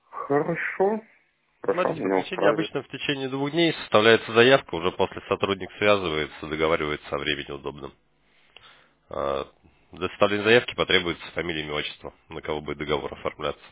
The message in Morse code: none

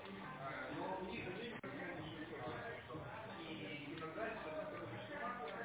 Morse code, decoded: none

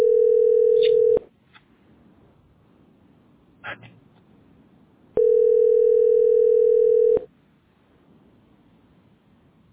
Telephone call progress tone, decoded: ringback tone

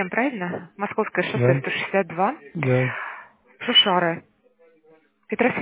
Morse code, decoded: none

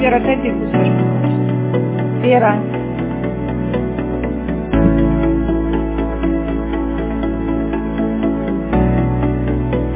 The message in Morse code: none

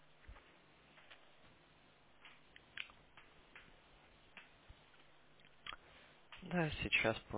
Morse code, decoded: none

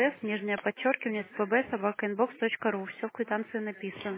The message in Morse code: none

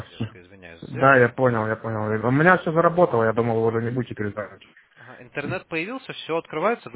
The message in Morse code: none

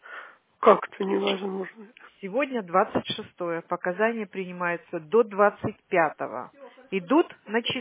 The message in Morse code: none

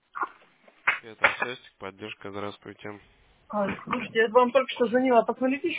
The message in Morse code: none

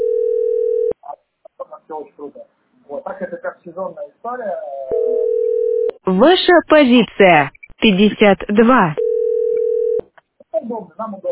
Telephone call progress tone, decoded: ringback tone

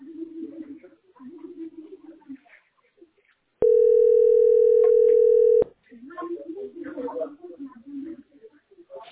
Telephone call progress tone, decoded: ringback tone

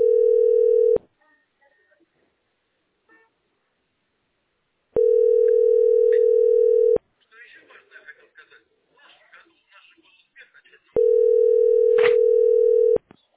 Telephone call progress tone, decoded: ringback tone

